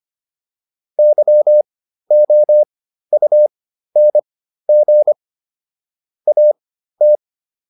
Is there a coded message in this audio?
Yes: Morse code from 1.0 to 7.2 s, heard as a keyed tone, about 25 wpm, spaced as for 13 wpm, 600 Hz. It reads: YOUNG AT